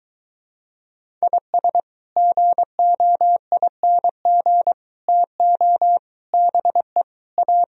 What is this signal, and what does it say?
Morse code, heard as a keyed tone, about 22 wpm, 700 Hz. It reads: IS GOING TO BE A